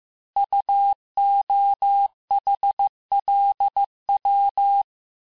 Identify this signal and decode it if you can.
Morse code, heard as a keyed tone, about 15 wpm, 800 Hz. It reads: UOHLW
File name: cours-du-lundi files / cours-1585-texte-chiffre-vitesse-14mpm